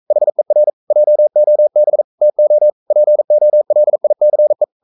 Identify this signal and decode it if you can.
Morse code, heard as a keyed tone, about 42 wpm, 600 Hz. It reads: HER JOB TO POLICE
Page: Romanen Never af Ken Follet på engelsk.